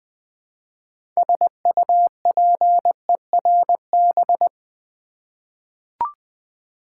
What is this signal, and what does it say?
Morse code, heard as a keyed tone, about 20 wpm, 700 Hz. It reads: SUPERB
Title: 6 Letter Words - First 3 Letters Given - 20wpm